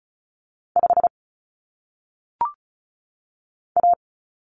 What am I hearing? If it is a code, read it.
Morse code, heard as a keyed tone, about 36 wpm, 700 Hz. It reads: 5 A
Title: Common Cut Numbers - 1-Digit 35wpm